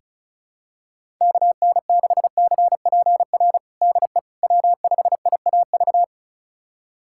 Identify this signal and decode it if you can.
Morse code, heard as a keyed tone, about 34 wpm, 700 Hz. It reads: KN6CPR DE W5IAV
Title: USA Callsign DE Callsign 35wpm